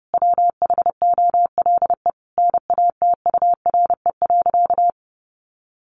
Morse code, heard as a keyed tone, about 30 wpm, 700 Hz. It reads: WHOLE NATURE.
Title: Sentences from Top 1500 Words 30wpm